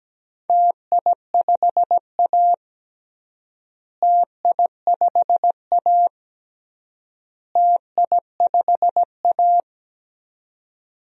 Morse code, heard as a keyed tone, about 17 wpm, 700 Hz. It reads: TI5A TI5A TI5A
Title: DXCC-100 Call Signs - 4 Characters - Repeated 3x - 17wpm